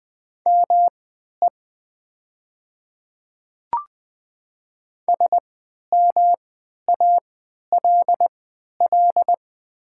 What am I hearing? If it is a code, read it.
Morse code, heard as a keyed tone, about 20 wpm, spaced as for 11 wpm, 700 Hz. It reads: ME SMALL
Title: 3x Character Spacing - Top 300 Words - 20wpm